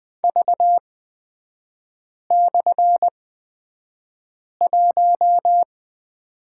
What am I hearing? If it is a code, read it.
Morse code, heard as a keyed tone, about 20 wpm, spaced as for 5 wpm, 700 Hz. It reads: V/1
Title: 250 Lesson 40e - BK - Three Characters 20wpm